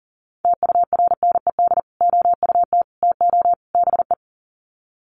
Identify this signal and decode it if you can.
Morse code, heard as a keyed tone, about 38 wpm, 700 Hz. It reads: TURNED OUT TO BE
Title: Sets of 4 Words 40wpm